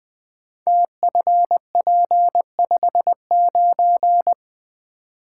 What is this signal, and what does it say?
Morse code, heard as a keyed tone, about 20 wpm, 700 Hz. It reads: TFP59
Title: Code Groups - 5 Letter:Numbers - 20wpm